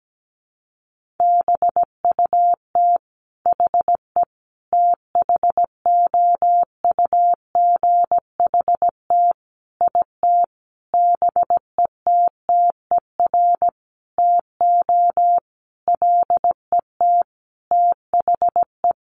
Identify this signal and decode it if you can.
Morse code, heard as a keyed tone, about 17 wpm, 700 Hz. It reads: BUT HE THOUGHT IT BETTER TO LET THE
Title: Sentences from Top 1400 Words 17wpm